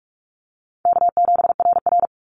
Morse code, heard as a keyed tone, about 44 wpm, 700 Hz. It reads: K7RR
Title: Easy Call Signs Repeated 2x - 45wpm